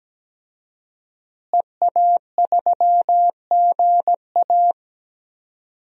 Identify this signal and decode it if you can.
Morse code, heard as a keyed tone, about 17 wpm, 700 Hz. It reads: EA3GA
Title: DXCC-100 Call Signs - 5 Characters - Repeated 3x - 17wpm